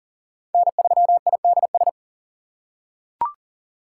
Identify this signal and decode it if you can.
Morse code, heard as a keyed tone, about 37 wpm, 700 Hz. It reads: N3IDS
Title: USA Call Signs - 5 Characters - 2x Character Spacing - 40wpm